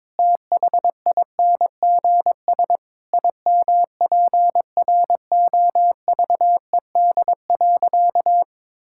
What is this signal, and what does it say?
Morse code, heard as a keyed tone, about 22 wpm, 700 Hz. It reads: THINGS IMPROVED.
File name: practice / 22wpm / Sentences with 4-Word Phrases - 22wpm